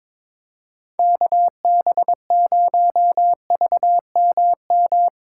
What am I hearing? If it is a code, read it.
Morse code, heard as a keyed tone, about 24 wpm, 700 Hz. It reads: KB0VMM
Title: Speed Racing - USA Call Signs 22wpm